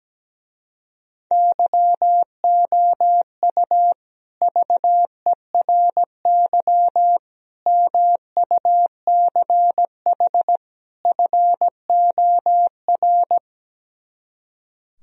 Morse code, heard as a keyed tone, about 17 wpm, 700 Hz. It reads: YOU VERY MUCH FOR
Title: Speed-Racing - Sets of 4 Words 17wpm